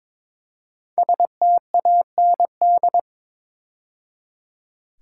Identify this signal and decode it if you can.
Morse code, heard as a keyed tone, about 22 wpm, 700 Hz. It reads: STAND
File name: 5 Letter Words - First 3 Letters Given - 22wpm